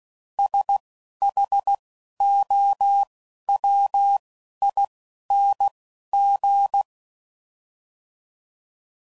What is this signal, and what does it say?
Morse code, heard as a keyed tone, about 16 wpm, 800 Hz. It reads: S H O W I N G
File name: Suffixes / Special INGs / Single INGs